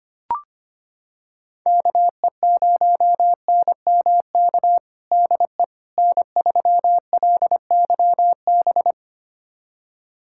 Morse code, heard as a keyed tone, about 25 wpm, 700 Hz. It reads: KE0NMK DE N3LYB
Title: USA Callsign DE Callsign 25wpm